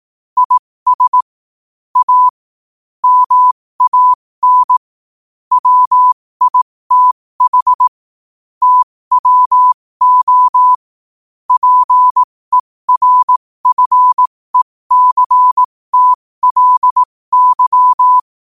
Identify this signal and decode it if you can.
Morse code, heard as a keyed tone, about 18 wpm, spaced as for 15 wpm, 1000 Hz. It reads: IS A MAN WITH TWO PERFECTLY
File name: qotdfiles / 15wpm